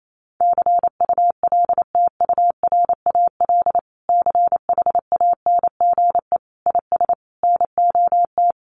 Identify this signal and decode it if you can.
Morse code, heard as a keyed tone, about 28 wpm, 700 Hz. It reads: CULTURAL CHANGE IS NOT
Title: Sentences from Top 1500 Words 28wpm